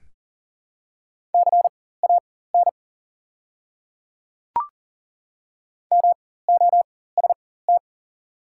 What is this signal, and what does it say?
Morse code, heard as a keyed tone, about 40 wpm, spaced as for 19 wpm, 700 Hz. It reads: CAN MOST
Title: 4x Character Spacing - Top 200 Words - 40wpm